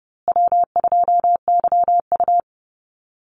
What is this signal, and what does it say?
Morse code, heard as a keyed tone, about 30 wpm, 700 Hz. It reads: W2YU
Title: USA Call Signs Repeated 3x 30wpm